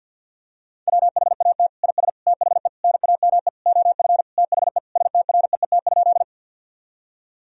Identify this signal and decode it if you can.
Morse code, heard as a keyed tone, about 50 wpm, 700 Hz. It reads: WHAT IS THE NAME OF THE STREET?
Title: Sentences from Top 1000 Words - Review Part 3 - 50wpm